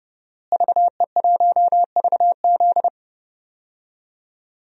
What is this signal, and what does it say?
Morse code, heard as a keyed tone, about 30 wpm, 700 Hz. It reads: VE1VZ
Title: DXCC-100 Call Signs - 5 Characters - Repeated 3x - 30wpm